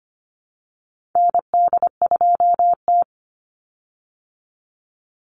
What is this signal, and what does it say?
Morse code, heard as a keyed tone, about 25 wpm, 700 Hz. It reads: ND2T
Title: Call Signs Repeated 2x - 25wpm